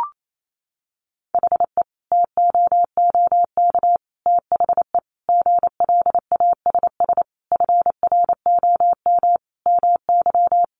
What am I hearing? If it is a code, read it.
Morse code, heard as a keyed tone, about 28 wpm, 700 Hz. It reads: HE TOOK THE GLASS FROM MY
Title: Sentences from Top 1300 Words 28wpm